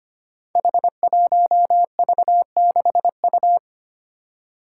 Morse code, heard as a keyed tone, about 25 wpm, 700 Hz. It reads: H1V6U
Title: Five Challenging Letter-Number Combinations 25wpm